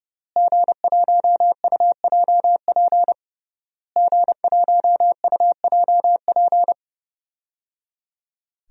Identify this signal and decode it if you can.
Morse code, heard as a keyed tone, about 30 wpm, 700 Hz. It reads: G1UJP G1UJP
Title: DXCC-100 Call Signs - 5 Characters - Repeated 3x - 30wpm